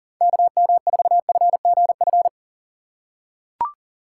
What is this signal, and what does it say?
Morse code, heard as a keyed tone, about 40 wpm, 700 Hz. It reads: KM4FGF
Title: USA Call Signs - 6 Characters - 2x Character Spacing - 40wpm